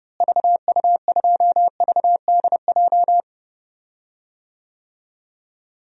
Morse code, heard as a keyed tone, about 30 wpm, 700 Hz. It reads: VU2VDJ
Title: DXCC-100 Call Signs - 6 Characters - Repeated 2x - 30wpm